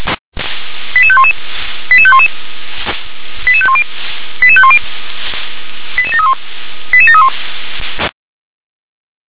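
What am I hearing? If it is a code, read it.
Five-tone selective call (ZVEI), heard as quick runs of five tones, tones of about 70 ms. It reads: 88511 88511 88510 88510 88731 88731
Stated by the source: Five-tone ZVEI selective call received on an SDR.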